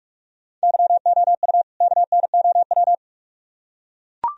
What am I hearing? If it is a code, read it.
Morse code, heard as a keyed tone, about 44 wpm, 700 Hz. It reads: YOU KNOW